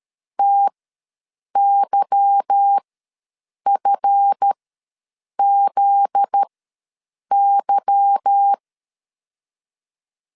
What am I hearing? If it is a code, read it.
Morse code, heard as a keyed tone, about 13 wpm, spaced as for 7 wpm, 800 Hz. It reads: TYFZY
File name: LEZIONI / 8 YJ45 / 8 Y ws 12wpm 3409